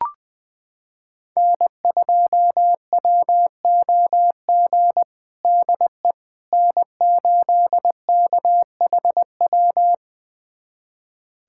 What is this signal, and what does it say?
Morse code, heard as a keyed tone, about 20 wpm, 700 Hz. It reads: N2WOG DE N8KHW
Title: USA Callsign DE Callsign - 5 Characters - 20wpm